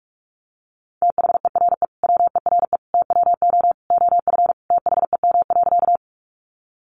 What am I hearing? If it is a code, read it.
Morse code, heard as a keyed tone, about 44 wpm, 700 Hz. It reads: THERE WERE TWO OF THEM.
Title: Sentences from Top 100 Words 45wpm